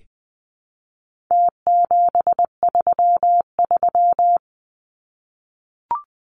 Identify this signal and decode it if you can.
Morse code, heard as a keyed tone, about 20 wpm, 700 Hz. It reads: T733
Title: Numbers - Contest Style - 4-Digit 20wpm